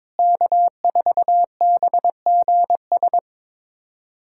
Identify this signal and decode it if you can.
Morse code, heard as a keyed tone, about 22 wpm, 700 Hz. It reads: K4BGS
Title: USA Callsign DE Callsign - 5 Characters - 22wpm